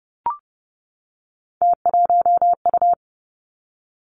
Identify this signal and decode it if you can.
Morse code, heard as a keyed tone, about 32 wpm, 700 Hz. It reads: T1U
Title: Three Challenging Letter-Number Combinations 30wpm